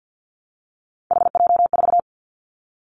Morse code, heard as a keyed tone, about 50 wpm, 700 Hz. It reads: HJ4
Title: Callsign Prefixes for Top 100 DXCC - 50wpm